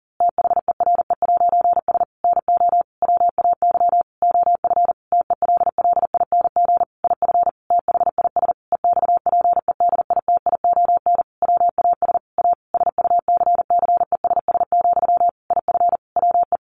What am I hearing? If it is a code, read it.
Morse code, heard as a keyed tone, about 40 wpm, 700 Hz. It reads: THERE'S NO WAY OF TELLING IF THIS EXPEDITION WAS A SUCCESS, IF WE
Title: A Sound Of Thunder 40wpm